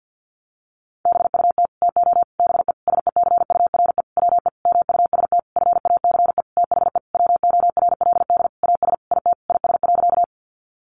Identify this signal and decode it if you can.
Morse code, heard as a keyed tone, about 49 wpm, 700 Hz. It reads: BUT TO BE SECURE WE MUST FACE THE WORLD AS IT IS.
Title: Sentences from Top 1200 Words 50wpm